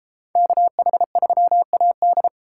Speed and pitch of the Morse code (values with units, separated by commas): 33 wpm, 700 Hz